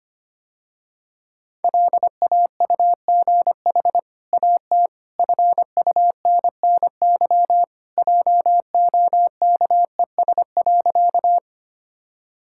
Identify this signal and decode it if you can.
Morse code, heard as a keyed tone, about 25 wpm, 700 Hz. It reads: LAUGH AT FUNNY JOKES.
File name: Easy Sentences with Hints - 25wpm